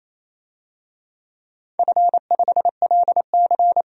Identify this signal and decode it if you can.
Morse code, heard as a keyed tone, about 28 wpm, 700 Hz. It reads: F5LC
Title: DXCC-100 Call Signs - 4 Characters - 28wpm